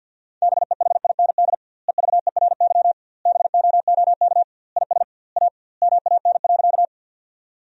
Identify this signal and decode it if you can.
Morse code, heard as a keyed tone, about 50 wpm, 700 Hz. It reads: BEHIND EVERY BOOK IS A MAN.
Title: 2x Word-Spacing - Sentences from Top 500 Words - 50wpm